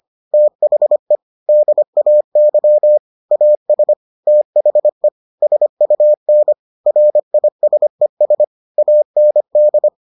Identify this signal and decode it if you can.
Morse code, heard as a keyed tone, about 25 wpm, 600 Hz. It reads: THE DAY AS THE SUN RISES AND